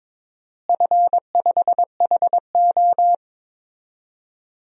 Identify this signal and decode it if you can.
Morse code, heard as a keyed tone, about 22 wpm, 700 Hz. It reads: F5HO
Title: DXCC-100 Call Signs - 4 Characters - Repeated 2x - 22wpm